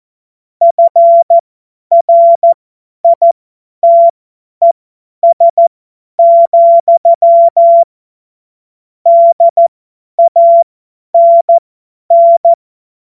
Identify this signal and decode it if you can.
Morse code, heard as a keyed tone, about 14 wpm, spaced as for 10 wpm, 680 Hz. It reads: FRITES, DANN
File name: Morsekurs / Witze / Witz 13. Januar 2026 50 F70